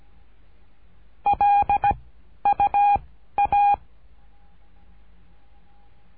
Morse code, heard as a keyed tone, about 17 wpm, 810 Hz. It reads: L U A
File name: cursos / CURSO DE TELEGRAFIA / Telegrafia - Trilha 13